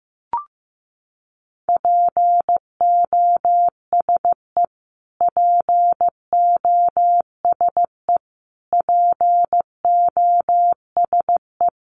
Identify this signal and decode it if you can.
Morse code, heard as a keyed tone, about 15 wpm, 700 Hz. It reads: POSE POSE POSE
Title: Top 500 Words Repeated 15wpm